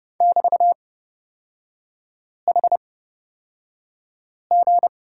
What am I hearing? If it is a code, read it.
Morse code, heard as a keyed tone, about 30 wpm, spaced as for 5 wpm, 700 Hz. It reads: <BT>HG
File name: Lesson 36e - BT - Three Characters 30wpm